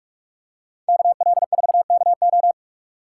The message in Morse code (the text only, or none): KL4KO